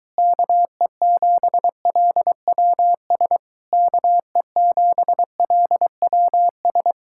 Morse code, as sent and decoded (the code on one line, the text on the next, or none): -.- . --... .-.. .-- ... / -.- . --... .-.. .-- ...
KE7LWS KE7LWS